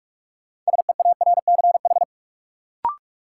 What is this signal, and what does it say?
Morse code, heard as a keyed tone, about 42 wpm, 700 Hz. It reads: SEARCH
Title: Sentences from Top 800 Words - New Word Drill - 45wpm